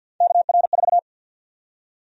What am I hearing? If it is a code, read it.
Morse code, heard as a keyed tone, about 52 wpm, 700 Hz. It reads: KR4
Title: USA Call Sign Prefixes - 50wpm